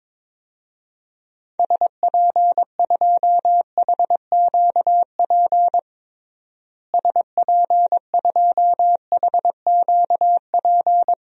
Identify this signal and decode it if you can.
Morse code, heard as a keyed tone, about 22 wpm, 700 Hz. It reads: SP2HQP SP2HQP
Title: DXCC-100 Call Signs - 6 Characters - Repeated 3x - 22wpm